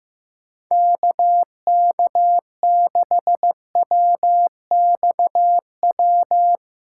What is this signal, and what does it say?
Morse code, heard as a keyed tone, about 16 wpm, 700 Hz. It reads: KK6WXW
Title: USA Callsign DE Callsign - 6 Characters - 15wpm